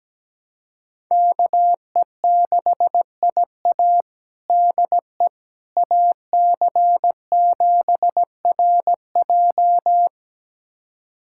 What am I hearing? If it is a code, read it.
Morse code, heard as a keyed tone, about 17 wpm, 700 Hz. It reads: KE6IA DE AC7RJ
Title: USA Callsign DE Callsign - 5 Characters - 17wpm